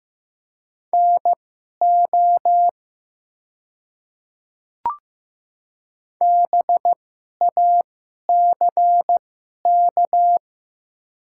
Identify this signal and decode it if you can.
Morse code, heard as a keyed tone, about 15 wpm, spaced as for 11 wpm, 700 Hz. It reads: NO BACK
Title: 2x Character Spacing - Top 200 Words - 15wpm